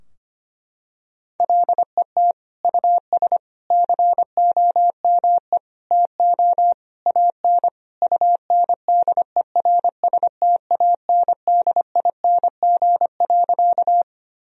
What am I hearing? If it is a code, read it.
Morse code, heard as a keyed tone, about 25 wpm, 700 Hz. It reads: LET US COME TO AN UNDERSTANDING.
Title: Sentences from Top 1000 Words - Review Part 3 - 25wpm